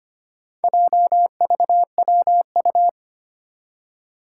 Morse code, heard as a keyed tone, about 25 wpm, 700 Hz. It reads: JVWU
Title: Four Challenging Letter-Number Combinations 25wpm